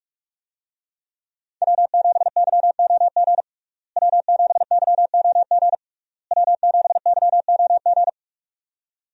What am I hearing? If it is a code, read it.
Morse code, heard as a keyed tone, about 42 wpm, 700 Hz. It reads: W7YOG W7YOG W7YOG
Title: USA Call Signs - 5 Characters - Repeated 3x - 45wpm